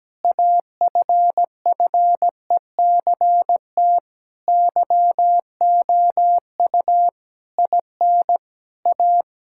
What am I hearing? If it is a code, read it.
Morse code, heard as a keyed tone, about 17 wpm, 700 Hz. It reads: AFFECT YOU IN A
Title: Sentences from Top 1000 Words - Review Part 6 - 17wpm